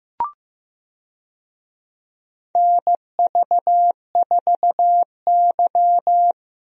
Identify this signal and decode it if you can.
Morse code, heard as a keyed tone, about 15 wpm, 700 Hz. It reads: NV4Y
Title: USA Call Signs - 4 Characters - Repeated 3x - 15wpm